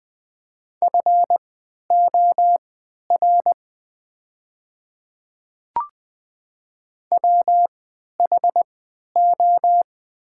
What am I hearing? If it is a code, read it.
Morse code, heard as a keyed tone, about 20 wpm, spaced as for 11 wpm, 700 Hz. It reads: FOR WHO
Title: 3x Character Spacing - Top 300 Words - 20wpm